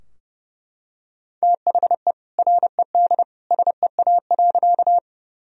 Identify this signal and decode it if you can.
Morse code, heard as a keyed tone, about 30 wpm, 700 Hz. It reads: THE RED SEA.
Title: Sentences from Top 1000 Words - Review Part 2 - 30wpm